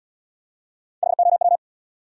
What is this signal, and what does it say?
Morse code, heard as a keyed tone, about 67 wpm, 700 Hz. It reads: HOW